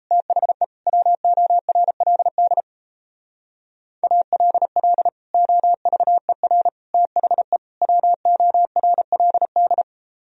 Morse code, heard as a keyed tone, about 33 wpm, 700 Hz. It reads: THE WORLD ALL OVER THE WORLD
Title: Speed-Racing - Sets of 4 Words 28wpm